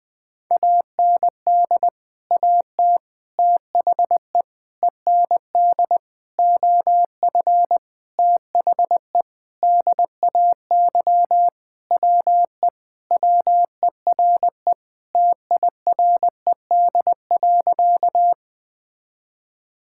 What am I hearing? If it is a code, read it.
Morse code, heard as a keyed tone, about 20 wpm, 700 Hz. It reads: AND AT THE END OF THE DAY WE WERE TIRED.
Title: Sentences with 5-Word Phrases - 20wpm